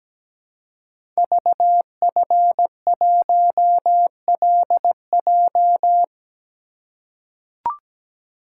Morse code, heard as a keyed tone, about 17 wpm, 690 Hz. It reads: VF1LJ
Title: Code Groups - 5 Letter:Numbers - 17wpm